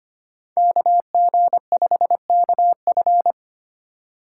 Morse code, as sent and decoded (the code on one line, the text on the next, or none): -.- --. ..... -.- ..-.
KG5KF